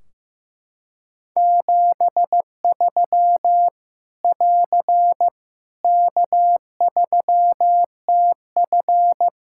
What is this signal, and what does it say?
Morse code, heard as a keyed tone, about 16 wpm, 700 Hz. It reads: 73 <AR> K3TF
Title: Practice QSO - Exchange 4 - Ending - 15wpm